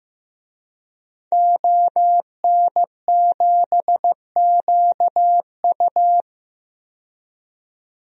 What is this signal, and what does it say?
Morse code, heard as a keyed tone, about 15 wpm, 700 Hz. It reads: ON7QU
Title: DXCC-100 Call Signs - 5 Characters - 15wpm